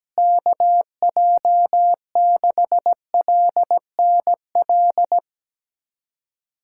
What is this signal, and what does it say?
Morse code, heard as a keyed tone, about 17 wpm, 700 Hz. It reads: KJ6LNL
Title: USA Callsign DE Callsign - 6 Characters - 17wpm